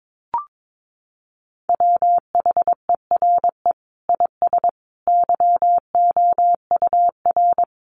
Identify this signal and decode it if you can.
Morse code, heard as a keyed tone, about 22 wpm, 700 Hz. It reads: WHERE IS YOUR
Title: Sentences from Top 1400 Words 22wpm